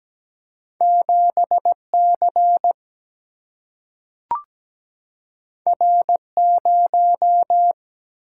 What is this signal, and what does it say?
Morse code, heard as a keyed tone, about 17 wpm, 700 Hz. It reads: 7C R0
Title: Single-Letter Single-Number 17wpm